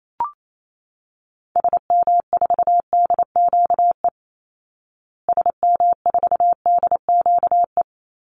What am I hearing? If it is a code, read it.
Morse code, heard as a keyed tone, about 28 wpm, 690 Hz. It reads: SM4DQE SM4DQE
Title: Call Signs Repeated 3x - 28wpm